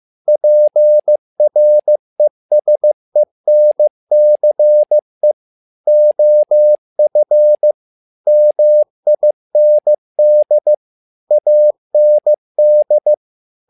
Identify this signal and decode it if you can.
Morse code, heard as a keyed tone, about 15 wpm, 600 Hz. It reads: PRESENCE OF MIND AND
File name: learning center / tower trouble / tower trouble 15wpm